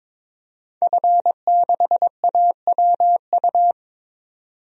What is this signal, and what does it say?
Morse code, heard as a keyed tone, about 22 wpm, 700 Hz. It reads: F6AWU
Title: DXCC-100 Call Signs - 5 Characters - 22wpm